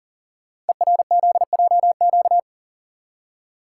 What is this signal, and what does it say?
Morse code, heard as a keyed tone, about 39 wpm, 700 Hz. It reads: ERZJQ
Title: Code Groups - 5 Letters - 40wpm